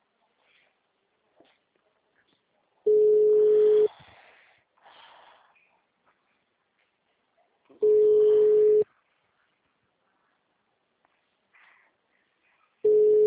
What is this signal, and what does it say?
Ringback tone (Europe, 425 Hz).